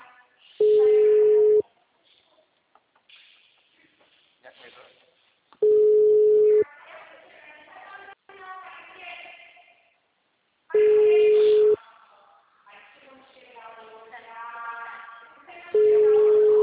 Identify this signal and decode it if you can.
Ringback tone (Europe, 425 Hz), with speech.
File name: out- -2022- - - .89551